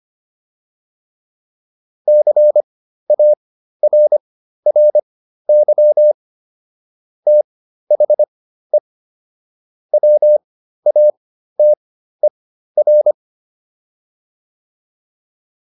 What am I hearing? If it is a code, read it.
Morse code, heard as a keyed tone, about 25 wpm, spaced as for 13 wpm, 600 Hz. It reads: CARRY THE WATER
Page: intermediate-practice-files